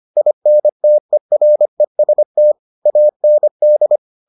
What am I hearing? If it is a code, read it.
Morse code, heard as a keyed tone, about 25 wpm, 600 Hz. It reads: INTEREST AND